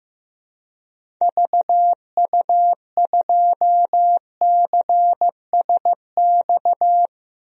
Morse code, heard as a keyed tone, about 16 wpm, 700 Hz. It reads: VU2CSX
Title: DXCC-100 Call Signs - 6 Characters - 15wpm